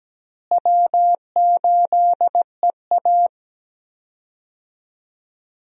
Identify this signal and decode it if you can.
Morse code, heard as a keyed tone, about 17 wpm, 700 Hz. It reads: W8EA
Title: USA Call Signs - 4 Characters - Repeated 3x - 17wpm